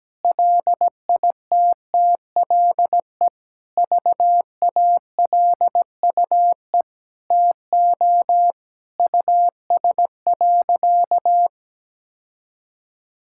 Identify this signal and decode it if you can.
Morse code, heard as a keyed tone, about 17 wpm, 700 Hz. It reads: LITTLE VALUE TO US.
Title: Sentences from Top 1000 Words - Review Part 5 - 17wpm